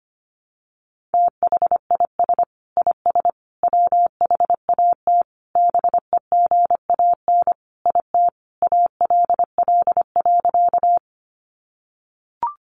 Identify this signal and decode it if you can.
Morse code, heard as a keyed tone, about 25 wpm, 700 Hz. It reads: THIS IS WHAT BEGAN IT ALL.
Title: Sentences from Top 300 Words 25wpm